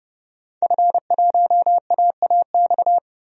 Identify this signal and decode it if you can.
Morse code, heard as a keyed tone, about 31 wpm, 700 Hz. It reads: F1AAX
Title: DXCC-100 Call Signs - 5 Characters - 30wpm